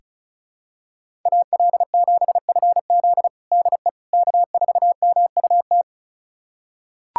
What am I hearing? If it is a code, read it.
Morse code, heard as a keyed tone, about 36 wpm, 700 Hz. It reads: AL7FZ DE K4MUT
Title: USA Callsign DE Callsign - 5 Characters - 35wpm